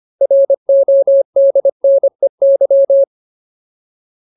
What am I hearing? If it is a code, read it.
Morse code, heard as a keyed tone, about 25 wpm, 550 Hz. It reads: RODNEY
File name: Ultimate Word List / 25WPM Ultimate